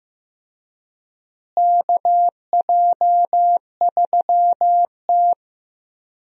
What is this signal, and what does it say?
Morse code, heard as a keyed tone, about 15 wpm, 700 Hz. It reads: KJ3T
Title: USA Call Signs - 4 Characters - Repeated 3x - 15wpm